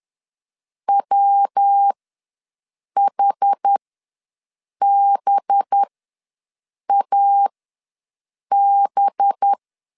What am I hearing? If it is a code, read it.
Morse code, heard as a keyed tone, about 10 wpm, spaced as for 6 wpm, 800 Hz. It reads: WHBAB
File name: LEZIONI / 5 LBVC / 5 LB ws 10wpm 3616